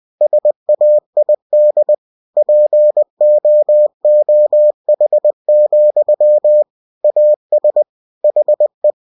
Morse code, heard as a keyed tone, about 20 wpm, 600 Hz. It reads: SAID POOH, AS HE
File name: cw-books / winnie-the-pooh / winnie-20wpm-page-0094